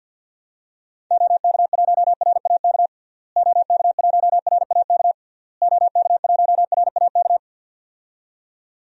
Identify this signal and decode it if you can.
Morse code, heard as a keyed tone, about 51 wpm, 700 Hz. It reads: OK1RAK OK1RAK OK1RAK
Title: DXCC-100 Call Signs - 6 Characters - Repeated 3x - 50wpm